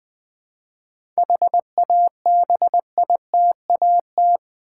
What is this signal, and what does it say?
Morse code, heard as a keyed tone, about 20 wpm, 700 Hz. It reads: HABITAT
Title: 7 Letter Words - First 3 Letters Given - 20wpm